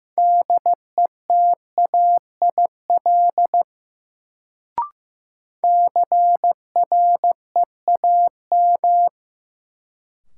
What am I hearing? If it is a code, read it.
Morse code, heard as a keyed tone, about 15 wpm, 700 Hz. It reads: DETAIL CREAM
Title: Sentences from Top 1500 Words - New Word Drill Alt - 15wpm